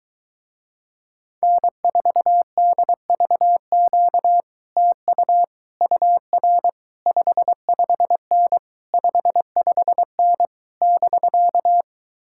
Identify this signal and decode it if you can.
Morse code, heard as a keyed tone, about 23 wpm, 700 Hz. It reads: N4DVQ TU UR 55N 55N <BK>